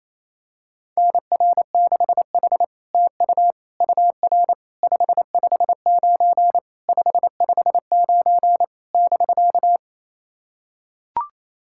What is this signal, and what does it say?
Morse code, heard as a keyed tone, about 29 wpm, 700 Hz. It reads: NR6H TU UR 559 559 <BK>
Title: SOTA QSO Practice 28wpm